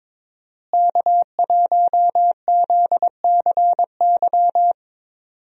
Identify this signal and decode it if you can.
Morse code, heard as a keyed tone, about 22 wpm, 700 Hz. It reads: K1ZCY
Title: USA Call Signs - 5 Characters - Repeated 3x - 22wpm